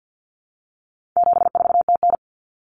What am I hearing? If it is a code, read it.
Morse code, heard as a keyed tone, about 50 wpm, 700 Hz. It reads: 74TN
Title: Common Cut Numbers - 4-Digit 50wpm